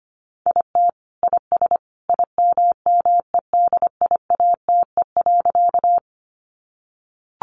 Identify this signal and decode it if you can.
Morse code, heard as a keyed tone, about 25 wpm, 700 Hz. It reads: IT IS IMMEDIATE.
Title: Sentences from Top 1200 Words 25wpm